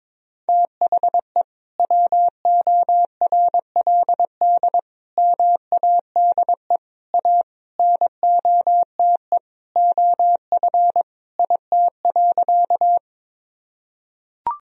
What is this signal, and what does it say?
Morse code, heard as a keyed tone, about 21 wpm, 700 Hz. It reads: THE WORLD MADE A NOTE OF IT.
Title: Sentences from Top 1000 Words - Review Part 2 - 22wpm